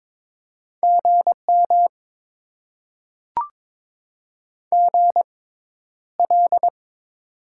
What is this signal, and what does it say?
Morse code, heard as a keyed tone, about 21 wpm, spaced as for 11 wpm, 700 Hz. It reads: GM GL
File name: Top 90 QSO Elements - 6x Character Spacing - 22wpm